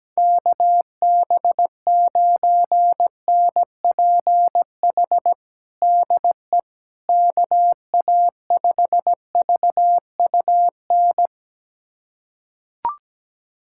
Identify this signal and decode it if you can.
Morse code, heard as a keyed tone, about 17 wpm, 700 Hz. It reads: KB9NPH DE KA5VUN
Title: USA Callsign DE Callsign 17wpm